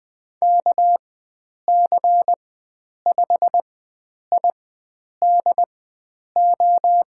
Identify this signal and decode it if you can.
Morse code, heard as a keyed tone, about 20 wpm, spaced as for 9 wpm, 700 Hz. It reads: KC5IDO
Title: USA Call Signs - 6 Characters - 4x Character Spacing - 20wpm